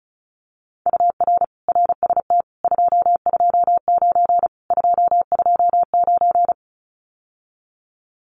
Morse code, heard as a keyed tone, about 34 wpm, 700 Hz. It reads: UR RST 229 229
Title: Common QSO Phrases Encore 35wpm